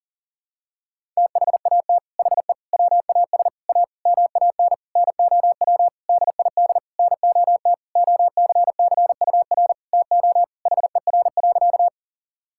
Morse code, heard as a keyed tone, about 40 wpm, 700 Hz. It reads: THAT HE WAS A MAN NOW DID NOT OCCUR TO HER.
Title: Sentences from Top 1000 Words - Review Part 6 - 40wpm